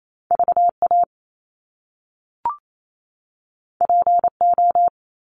Morse code, heard as a keyed tone, about 28 wpm, 700 Hz. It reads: V A P O
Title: Top Two Letter Combinations in Words 28wpm